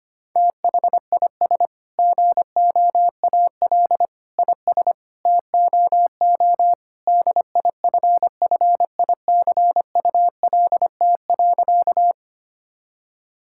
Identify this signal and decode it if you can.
Morse code, heard as a keyed tone, about 25 wpm, 700 Hz. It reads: THIS GOAL IS TOO DIFFICULT.
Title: Sentences from Top 1000 Words - Review Part 5 - 25wpm